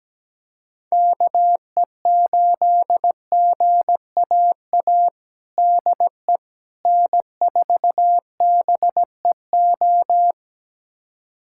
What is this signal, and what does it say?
Morse code, heard as a keyed tone, about 17 wpm, 700 Hz. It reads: KE8GAA DE N4BEO
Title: USA Callsign DE Callsign 17wpm